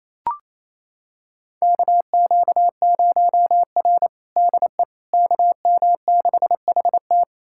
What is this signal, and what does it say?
Morse code, heard as a keyed tone, about 28 wpm, 700 Hz. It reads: KQ0R DE KM6HT
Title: USA Callsign DE Callsign 28wpm